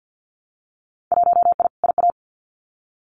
Morse code, heard as a keyed tone, about 48 wpm, 700 Hz. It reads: 1I IA
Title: Winter Field Day Exchanges 50wpm